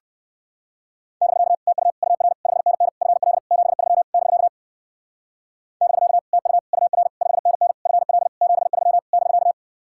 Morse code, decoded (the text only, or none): <BK> TU UR 5NN FL 73 <BK> <BK> TU UR 5NN FL 73 <BK>